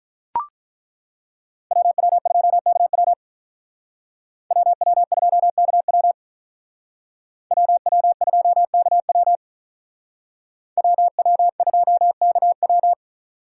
Morse code, heard as a keyed tone, about 42 wpm, 700 Hz. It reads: WW2KW WW2KW WW2KW WW2KW
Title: Speed Racing - USA 2x2 Call Signs 35wpm